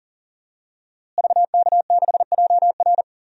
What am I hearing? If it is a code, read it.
Morse code, heard as a keyed tone, about 40 wpm, 700 Hz. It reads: VK6JR
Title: DXCC-100 Call Signs - 5 Characters - Repeated 2x - 40wpm